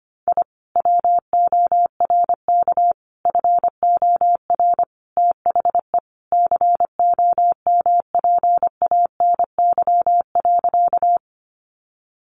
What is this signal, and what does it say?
Morse code, heard as a keyed tone, about 25 wpm, 700 Hz. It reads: I WORK FOR THE COMPANY.
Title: Sentences from Top 600 Words 25wpm